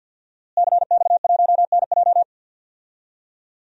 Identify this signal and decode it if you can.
Morse code, heard as a keyed tone, about 53 wpm, 700 Hz. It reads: CX1NJ
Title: DXCC-100 Call Signs - 5 Characters - Repeated 2x - 50wpm